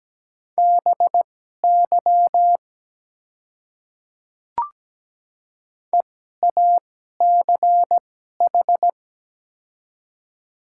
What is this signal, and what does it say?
Morse code, heard as a keyed tone, about 17 wpm, spaced as for 12 wpm, 700 Hz. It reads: BY EACH
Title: 2x Character Spacing - Top 100 Words - 17wpm